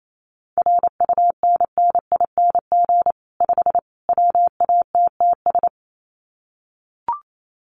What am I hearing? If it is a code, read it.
Morse code, heard as a keyed tone, about 28 wpm, 700 Hz. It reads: RUNNING 5 WATTS
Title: Common QSO Phrases 28wpm